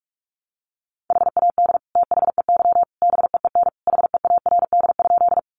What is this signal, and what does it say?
Morse code, heard as a keyed tone, about 44 wpm, 700 Hz. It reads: HAD THEY BEEN HEARD?
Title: Sentences from Top 400 Words 45wpm